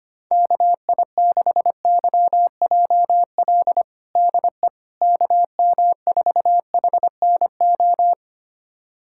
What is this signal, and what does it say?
Morse code, heard as a keyed tone, about 25 wpm, 700 Hz. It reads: KI6YJL DE KM4HNO
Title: USA Callsign DE Callsign - 6 Characters - 25wpm